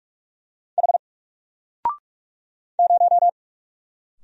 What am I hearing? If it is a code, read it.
Morse code, heard as a keyed tone, about 45 wpm, 700 Hz. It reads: H 0
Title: Single Letter-Number 45wpm